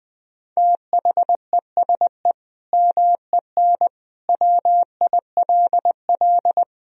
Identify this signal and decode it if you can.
Morse code, heard as a keyed tone, about 20 wpm, 700 Hz. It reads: THESE MEN WILL
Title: Sentences from Top 1000 Words - Review Part 1 - 20wpm